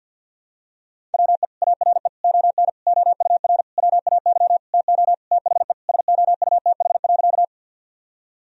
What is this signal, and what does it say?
Morse code, heard as a keyed tone, about 52 wpm, 700 Hz. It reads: WE ARE ON OUR WAY TO THE SOUTH.
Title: Sentences from Top 1000 Words - Review Part 2 - 50wpm